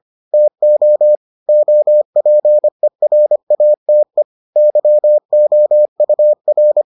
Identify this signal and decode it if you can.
Morse code, heard as a keyed tone, about 25 wpm, 600 Hz. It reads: TO OPERATE YOUR